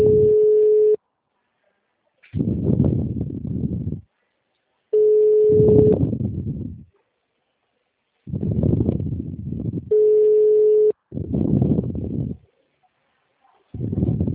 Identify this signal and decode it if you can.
Ringback tone (Europe, 425 Hz).
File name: out- -2009- - - .82